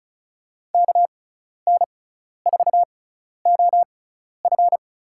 Morse code, heard as a keyed tone, about 36 wpm, spaced as for 12 wpm, 700 Hz. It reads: KN4OF
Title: USA Call Signs - 6x Character Spacing - 35wpm